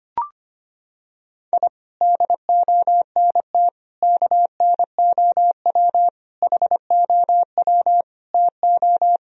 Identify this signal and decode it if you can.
Morse code, heard as a keyed tone, about 25 wpm, 700 Hz. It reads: I DONT KNOW HOW TO
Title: Sentences from Top 1000 Words - Review Part 1 - 25wpm